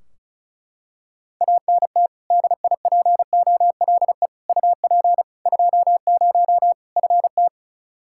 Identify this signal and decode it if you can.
Morse code, heard as a keyed tone, about 34 wpm, 700 Hz. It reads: ANT DIPOLE UP 20 FT
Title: Practice QSO - Exchange 2 - Info - 35wpm